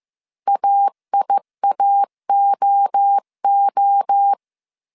Morse code, heard as a keyed tone, about 15 wpm, 800 Hz. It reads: AIAOO